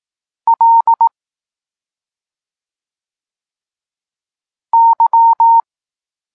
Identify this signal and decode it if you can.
Morse code, heard as a keyed tone, about 18 wpm, 920 Hz. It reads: L Y
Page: Tekst 32 – Doskonalenie odbioru poznanych liter